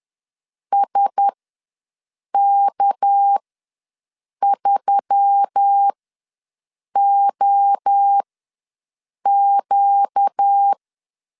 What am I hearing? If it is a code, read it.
Morse code, heard as a keyed tone, about 10 wpm, spaced as for 6 wpm, 800 Hz. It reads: SK3OQ
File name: LEZIONI / 6 PKX0 / 6 PKX ws 10wpm 3706